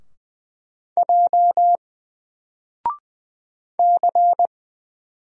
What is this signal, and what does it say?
Morse code, heard as a keyed tone, about 20 wpm, 700 Hz. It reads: J C
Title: Single Character (Letter:Number:Punctuation:Prosign) 20wpm